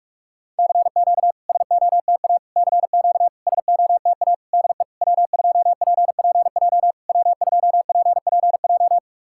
Morse code, heard as a keyed tone, about 46 wpm, 690 Hz. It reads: CQ SOTA CQ SOTA DE W2PPJ W2PPJ
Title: SOTA QSO Practice 45wpm